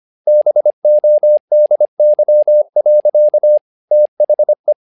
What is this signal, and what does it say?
Morse code, heard as a keyed tone, about 25 wpm, 600 Hz. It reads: BODY. THE
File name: learning center / tower trouble / tower trouble 25wpm